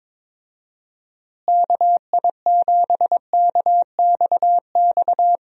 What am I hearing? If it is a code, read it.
Morse code, heard as a keyed tone, about 22 wpm, 700 Hz. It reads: KI7KXX
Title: USA Call Signs - 6 Characters - Repeated 3x - 22wpm